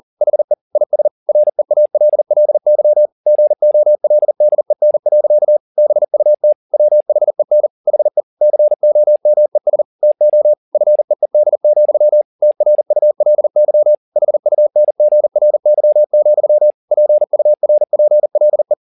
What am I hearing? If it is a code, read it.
Morse code, heard as a keyed tone, about 40 wpm, 600 Hz. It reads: HE IS REALLY GOLDEN. BUT WHEN HE COMES TO FEED, TRULY HUNGRY, PURPLE
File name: the old man and the sea 2 40wpm